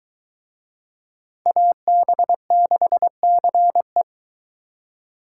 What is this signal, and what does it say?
Morse code, heard as a keyed tone, about 23 wpm, 700 Hz. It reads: AB6CE